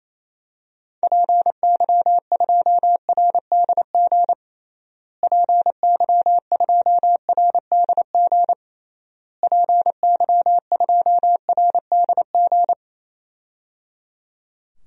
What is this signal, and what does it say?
Morse code, heard as a keyed tone, about 28 wpm, 700 Hz. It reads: PY2RDG PY2RDG PY2RDG
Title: DXCC-100 Call Signs - 6 Characters - Repeated 3x - 28wpm